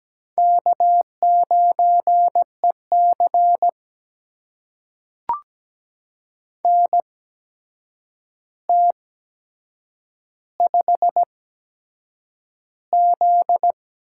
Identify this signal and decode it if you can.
Morse code, heard as a keyed tone, about 17 wpm, spaced as for 7 wpm, 700 Hz. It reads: K9EC NT5Z
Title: USA Call Signs - 4 Characters - 8x Character Spacing - 17wpm